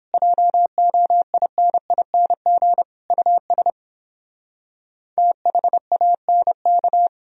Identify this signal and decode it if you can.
Morse code, heard as a keyed tone, about 29 wpm, 700 Hz. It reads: JOINING US THANK